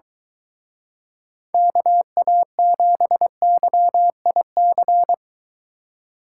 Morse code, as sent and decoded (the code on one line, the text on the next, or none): -.- .- --... -.-- .. -.-.
KA7YIC